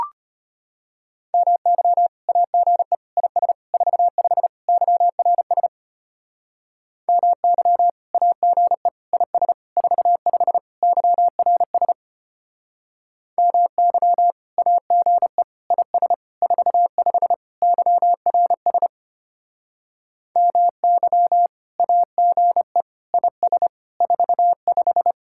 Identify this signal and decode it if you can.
Morse code, heard as a keyed tone, about 32 wpm, 700 Hz. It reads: MY AGE IS 45 YRS MY AGE IS 45 YRS MY AGE IS 45 YRS MY AGE IS 45